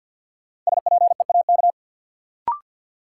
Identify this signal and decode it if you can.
Morse code, heard as a keyed tone, about 48 wpm, 700 Hz. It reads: SPEAK
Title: Sentences from Top 1000 Words - Review Part 2 - New Word Drill - 50wpm